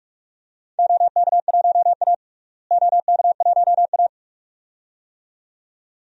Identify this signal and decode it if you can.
Morse code, heard as a keyed tone, about 46 wpm, 700 Hz. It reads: OK1A OK1A
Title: DXCC-100 Call Signs - 4 Characters - Repeated 2x - 45wpm